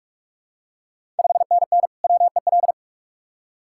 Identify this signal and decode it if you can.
Morse code, heard as a keyed tone, about 45 wpm, 690 Hz. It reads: 5NN WEL
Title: Ontario QSO Party Exchange 45wpm